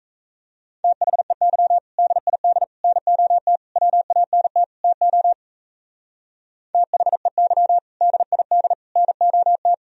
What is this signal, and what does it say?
Morse code, heard as a keyed tone, about 40 wpm, 700 Hz. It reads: THEY DID NOT WANT TO THEY DID NOT